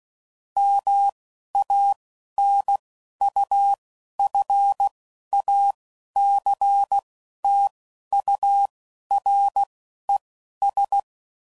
Morse code, heard as a keyed tone, about 16 wpm, spaced as for 11 wpm, 800 Hz. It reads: MANUFACTURES